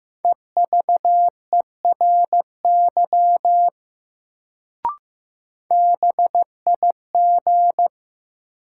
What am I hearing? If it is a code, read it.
Morse code, heard as a keyed tone, about 15 wpm, 700 Hz. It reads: EVERY BIG
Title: Sentences from Top 200 Words - New Word Drill - 15wpm